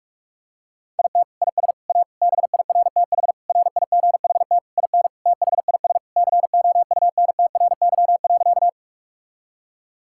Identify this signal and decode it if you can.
Morse code, heard as a keyed tone, about 46 wpm, 700 Hz. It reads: IT IS A BIRTH RIGHT IN THIS COUNTRY.
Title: Sentences from Top 1000 Words - Review Part 5 - 45wpm